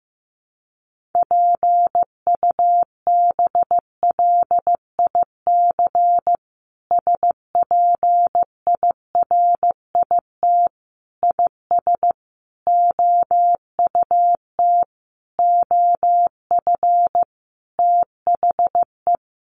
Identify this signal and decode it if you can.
Morse code, heard as a keyed tone, about 15 wpm, 700 Hz. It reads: PUBLIC SPIRIT IS OUT OF THE
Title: Sentences from Top 1400 Words 15wpm